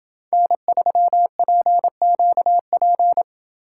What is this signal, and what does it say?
Morse code, heard as a keyed tone, about 27 wpm, 700 Hz. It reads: N3PQP